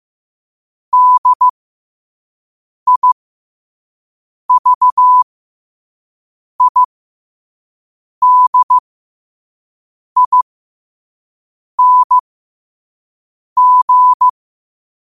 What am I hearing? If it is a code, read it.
Morse code, heard as a keyed tone, about 15 wpm, spaced as for 5 wpm, 1000 Hz. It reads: DIVIDING
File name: qotdfiles / 5wpm